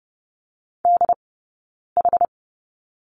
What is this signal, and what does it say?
Morse code, heard as a keyed tone, about 30 wpm, spaced as for 9 wpm, 690 Hz. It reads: DH
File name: course / 30:10wpm / 086 Lesson 15d - Number 5 - Two Characters 30wpm